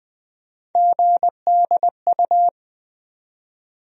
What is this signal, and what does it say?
Morse code, heard as a keyed tone, about 20 wpm, 700 Hz. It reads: GDU
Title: USA Call Sign Postfixes - 3 Letters 20wpm